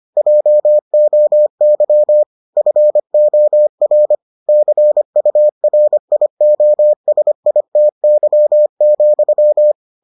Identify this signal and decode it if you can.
Morse code, heard as a keyed tone, about 25 wpm, 600 Hz. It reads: JOY FOR CURIOSITY,